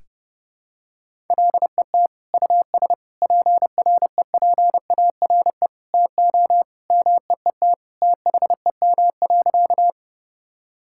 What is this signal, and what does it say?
Morse code, heard as a keyed tone, about 32 wpm, 700 Hz. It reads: LET US PREPARE TO MEET THEM.
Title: Sentences from Top 1000 Words - Review Part 6 - 30wpm